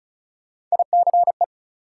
Morse code, heard as a keyed tone, about 36 wpm, 700 Hz. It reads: ICE
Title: Sentences from Top 1000 Words - New Word Drill - 35wpm